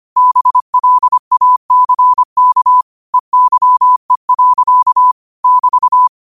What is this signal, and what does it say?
Morse code, heard as a keyed tone, about 25 wpm, 1000 Hz. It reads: BLACK EYE. <BT>